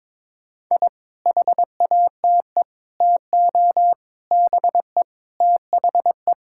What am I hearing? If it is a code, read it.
Morse code, heard as a keyed tone, about 22 wpm, 700 Hz. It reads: I HATE TO BE THE
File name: Sentences with 9-Word Phrases - 22wpm